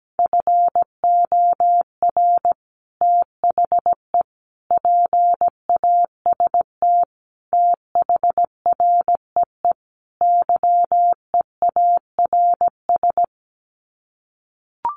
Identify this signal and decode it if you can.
Morse code, heard as a keyed tone, about 17 wpm, 700 Hz. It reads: FOR THE PAST THREE YEARS
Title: Sets of 5 Words Encore2 17wpm